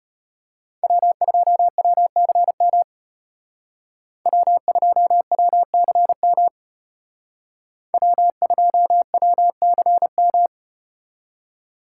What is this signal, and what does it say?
Morse code, heard as a keyed tone, about 32 wpm, 700 Hz. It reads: W2WCM W2WCM W2WCM